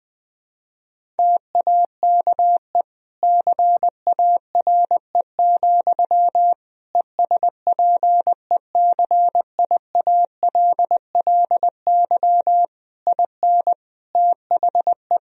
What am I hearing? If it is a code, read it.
Morse code, heard as a keyed tone, about 19 wpm, 700 Hz. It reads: TAKE CARE, ESPECIALLY IN THE
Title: Sentences from Top 1000 Words - Review Part 4 - 20wpm